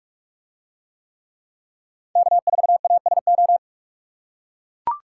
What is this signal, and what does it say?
Morse code, heard as a keyed tone, about 44 wpm, 700 Hz. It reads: K4ASO